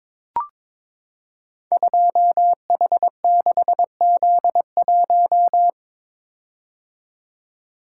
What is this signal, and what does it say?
Morse code, heard as a keyed tone, about 22 wpm, 700 Hz. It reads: 2H6Z1
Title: Code Groups - 5 Letter:Numbers - 22wpm